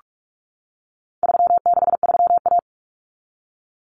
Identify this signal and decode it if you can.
Morse code, heard as a keyed tone, about 44 wpm, 700 Hz. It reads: 363A
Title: Common Cut Numbers - 4-Digit 45wpm